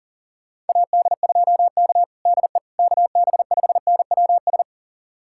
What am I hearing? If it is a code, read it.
Morse code, heard as a keyed tone, about 37 wpm, 700 Hz. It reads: AD2K DE KB5NWS
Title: USA Callsign DE Callsign 40wpm